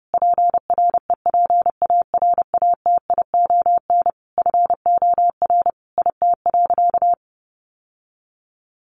Morse code, heard as a keyed tone, about 30 wpm, 700 Hz. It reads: PREPARATION FOR IT.
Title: Sentences from Top 1400 Words 30wpm